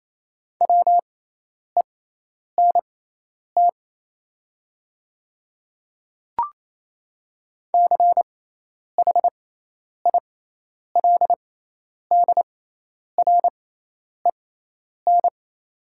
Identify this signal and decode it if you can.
Morse code, heard as a keyed tone, about 28 wpm, spaced as for 10 wpm, 700 Hz. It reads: WENT CHILDREN
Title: 6x Character Spacing - Top 500 Words - 28wpm